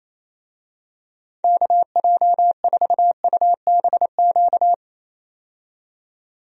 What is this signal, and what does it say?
Morse code, heard as a keyed tone, about 28 wpm, 700 Hz. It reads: KJ4UBQ
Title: USA Call Signs - 6 Characters - 28wpm